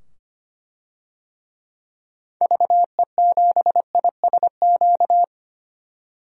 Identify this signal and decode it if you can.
Morse code, heard as a keyed tone, about 25 wpm, 700 Hz. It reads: VE7ISQ
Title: DXCC-100 Call Signs - 6 Characters - 25wpm